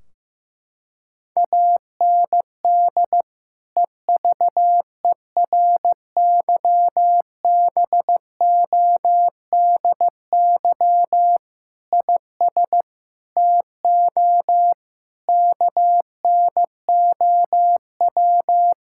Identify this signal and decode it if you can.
Morse code, heard as a keyed tone, about 15 wpm, 700 Hz. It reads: AND EVERYBODY IS TO KNOW
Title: Sentences from Top 1600 Words 15wpm